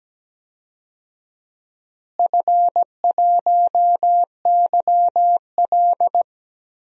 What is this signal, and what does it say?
Morse code, heard as a keyed tone, about 17 wpm, 700 Hz. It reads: F1YL
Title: DXCC-100 Call Signs - 4 Characters - Repeated 3x - 17wpm